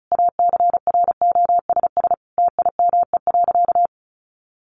Morse code, heard as a keyed tone, about 34 wpm, 700 Hz. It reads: ACROSS TIME.
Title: A Sound Of Thunder 35wpm